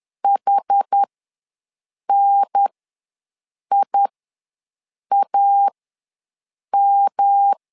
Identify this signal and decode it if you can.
Morse code, heard as a keyed tone, about 10 wpm, spaced as for 6 wpm, 800 Hz. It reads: HNIAM
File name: LEZIONI / 3 MWRO / 3 M ws 10wpm 2121